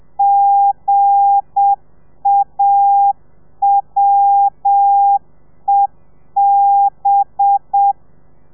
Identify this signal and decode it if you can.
Morse code, heard as a keyed tone, about 7 wpm, 800 Hz. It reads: GAWEB
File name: telegrafia / i2tfj / Lez8 30